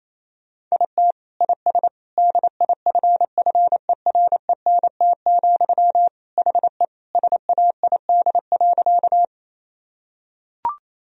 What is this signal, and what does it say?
Morse code, heard as a keyed tone, about 28 wpm, 700 Hz. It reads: IT IS DIFFERENT, HE SAID.
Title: Sentences from Top 1000 Words - Review Part 3 - 28wpm